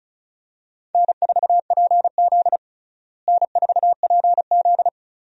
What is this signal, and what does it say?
Morse code, heard as a keyed tone, about 36 wpm, 700 Hz. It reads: N4PZ N4PZ
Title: USA Call Signs - 4 Characters - Repeated 2x - 35wpm